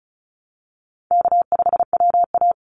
Morse code, heard as a keyed tone, about 34 wpm, 700 Hz. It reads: K5WA
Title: Easy Call Signs Repeated 2x - 35wpm